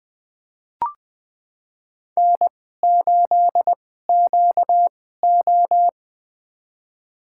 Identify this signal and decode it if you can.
Morse code, heard as a keyed tone, about 20 wpm, spaced as for 15 wpm, 700 Hz. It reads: N8QO
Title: USA Call Signs - 4 Characters - 2x Character Spacing - 20wpm